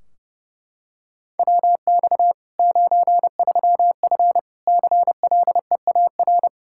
Morse code, heard as a keyed tone, about 30 wpm, 700 Hz. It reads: WX 93F CLEAR
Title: Practice QSO - Exchange 2 - Info - 30wpm